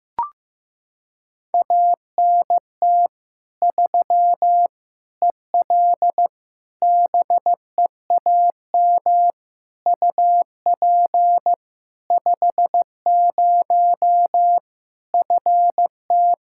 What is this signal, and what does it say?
Morse code, heard as a keyed tone, about 15 wpm, 700 Hz. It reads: ANT 3 EL BEAM UP 50 FT
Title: Practice QSO - Exchange 2 - Info - 15wpm